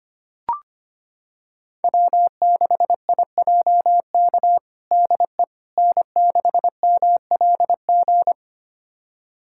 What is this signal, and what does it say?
Morse code, heard as a keyed tone, about 25 wpm, 700 Hz. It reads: W6IJK DE N6MLG
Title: USA Callsign DE Callsign - 5 Characters - 25wpm